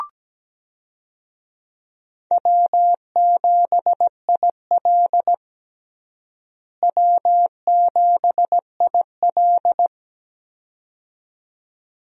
Morse code, heard as a keyed tone, about 17 wpm, 700 Hz. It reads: W7IL W7IL
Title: USA Call Signs - 4 Characters - Repeated 2x - 17wpm